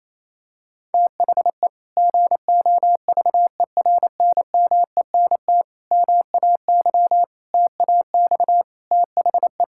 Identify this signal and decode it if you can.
Morse code, heard as a keyed tone, about 28 wpm, 700 Hz. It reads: THE GOVERNMENT MAY TAX THE
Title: Sentences from Top 1000 Words - Review Part 5 - 28wpm